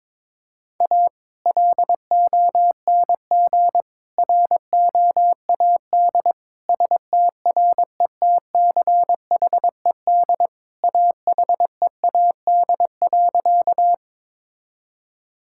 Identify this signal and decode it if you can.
Morse code, heard as a keyed tone, about 22 wpm, 700 Hz. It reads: A LONG ROAD STRETCHED AHEAD.
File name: practice / 22wpm / Sentences with 2-Word Phrases - 22wpm